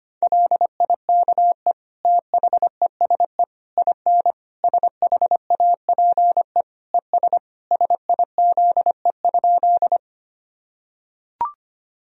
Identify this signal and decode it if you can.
Morse code, heard as a keyed tone, about 25 wpm, 700 Hz. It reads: LIKE THESE IN SHAPE ES SIZE?
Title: Sentences from Top 500 Words and Abbreviations 25wpm